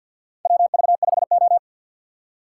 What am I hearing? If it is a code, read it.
Morse code, heard as a keyed tone, about 53 wpm, 700 Hz. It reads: WV5O
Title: USA Call Signs - 4 Characters - 3x Character Spacing - 50wpm